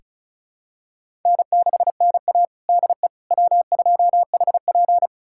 Morse code, heard as a keyed tone, about 36 wpm, 700 Hz. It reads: N6NA DE W2HP
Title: USA Callsign DE Callsign - 4 Characters - 35wpm